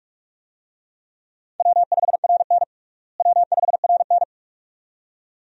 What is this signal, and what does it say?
Morse code, heard as a keyed tone, about 47 wpm, 700 Hz. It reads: W5RN W5RN
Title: USA Call Signs - 4 Characters - Repeated 2x - 45wpm